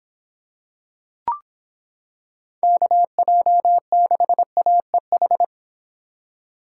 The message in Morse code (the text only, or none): KJ6AEH